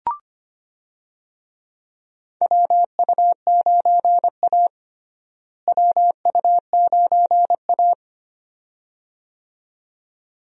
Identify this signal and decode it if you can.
Morse code, heard as a keyed tone, about 25 wpm, 700 Hz. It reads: WU9A WU9A
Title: USA Call Signs - 4 Characters - Repeated 2x - 25wpm